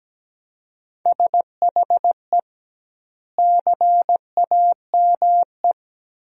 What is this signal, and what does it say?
Morse code, heard as a keyed tone, about 17 wpm, 700 Hz. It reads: SHE CAME
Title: 2x Word-Spacing - Sentences from Top 300 Words - 17wpm